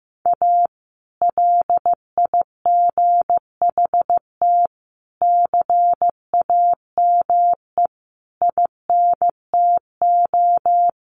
Morse code, heard as a keyed tone, about 15 wpm, 700 Hz. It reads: A LIGHT CAME INTO
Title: Sentences from Top 1100 Words 15wpm